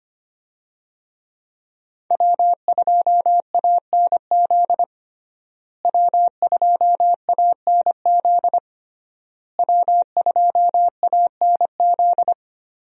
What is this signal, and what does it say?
Morse code, heard as a keyed tone, about 25 wpm, 700 Hz. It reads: W2ANZ W2ANZ W2ANZ
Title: USA Call Signs - 5 Characters - Repeated 3x - 25wpm